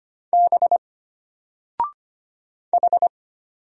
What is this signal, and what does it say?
Morse code, heard as a keyed tone, about 25 wpm, 700 Hz. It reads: B H
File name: Single-Letter-Number-25wpm